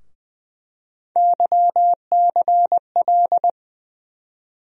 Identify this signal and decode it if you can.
Morse code, heard as a keyed tone, about 20 wpm, 700 Hz. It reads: YCL
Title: USA Call Sign Postfixes - 3 Letters 20wpm